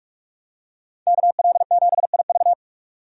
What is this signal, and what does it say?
Morse code, heard as a keyed tone, about 44 wpm, 700 Hz. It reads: KL7IV
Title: DXCC-100 Call Signs - 5 Characters - 45wpm